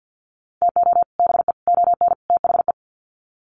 Morse code, heard as a keyed tone, about 50 wpm, 700 Hz. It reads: TO BE ON THE
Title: Sets of 4 Words 50wpm